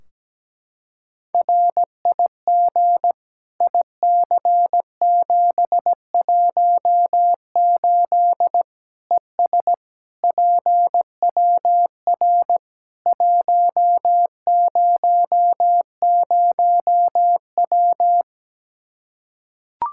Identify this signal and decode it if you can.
Morse code, heard as a keyed tone, about 17 wpm, 700 Hz. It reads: RIG IC718 ES PWR 100W
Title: Practice QSO - Exchange 2 - Info - 17wpm